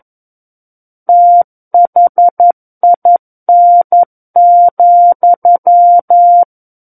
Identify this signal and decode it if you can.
Morse code, heard as a keyed tone, about 11 wpm, 700 Hz. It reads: THIN,
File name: Book One / 02-Chapter 2 - 11 WPM